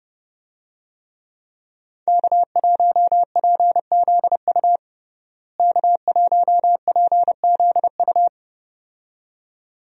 Morse code, heard as a keyed tone, about 30 wpm, 700 Hz. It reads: K1PZU K1PZU
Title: USA Call Signs - 5 Characters - Repeated 2x - 30wpm